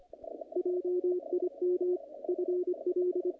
Morse code, heard as a keyed tone, about 25 wpm, 360 Hz, with radio noise. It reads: JIM FL
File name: contests / naqp-cw-k5tr-op-k5pi- -2000